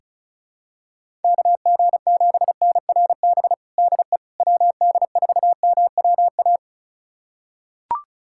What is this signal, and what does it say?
Morse code, heard as a keyed tone, about 36 wpm, 700 Hz. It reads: KG7NRB DE WD4MWA
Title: USA Callsign DE Callsign 35wpm